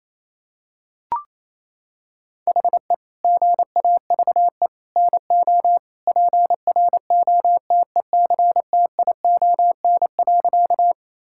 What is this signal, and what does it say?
Morse code, heard as a keyed tone, about 28 wpm, 700 Hz. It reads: HE GAVE NO PROTECTION.
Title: Sentences from Top 1600 Words 28wpm